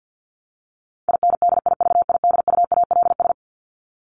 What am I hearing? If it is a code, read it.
Morse code, heard as a keyed tone, about 50 wpm, 700 Hz. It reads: INDIVIDUALS
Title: ES:S Suffix - Top 100 Words with Base Words 50wpm